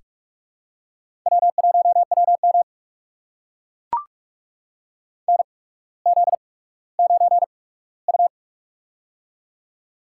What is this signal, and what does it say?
Morse code, heard as a keyed tone, about 44 wpm, spaced as for 19 wpm, 700 Hz. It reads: W1WM NZ9U
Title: USA Call Signs - 4 Characters - 8x Character Spacing - 45wpm